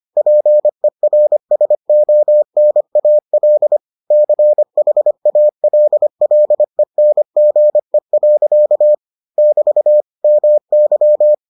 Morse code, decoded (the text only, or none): PERSONAL CHALLENGE. <BT> MY